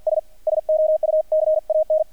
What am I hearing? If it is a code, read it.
Morse code, heard as a keyed tone, about 50 wpm, 620 Hz. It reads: S SO U CAN
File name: archived / CRSnet